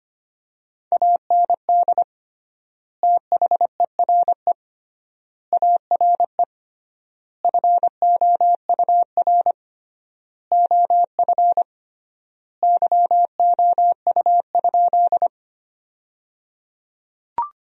Morse code, heard as a keyed tone, about 25 wpm, 700 Hz. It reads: AND THERE ARE FOUR OF YOU?
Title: 3x Word-Spacing - Sentences from Top 300 Words - 25wpm